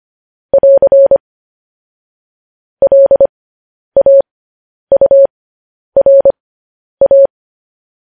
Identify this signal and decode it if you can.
Morse code, heard as a keyed tone, about 25 wpm, spaced as for 10 wpm, 550 Hz. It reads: <AR> LAURA
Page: news headlines in morse code at 10 wpm